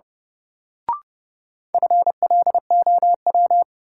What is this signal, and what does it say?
Morse code, heard as a keyed tone, about 30 wpm, 700 Hz. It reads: FLOW
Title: Sentences from Top 1200 Words - New Word Drill Alt - 30wpm